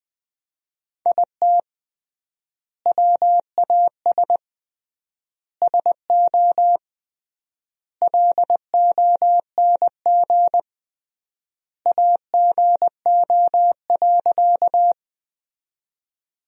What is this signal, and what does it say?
Morse code, heard as a keyed tone, about 20 wpm, 700 Hz. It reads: IT WAS SO LONG AGO.
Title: 3x Word-Spacing - Sentences from Top 500 Words - 20wpm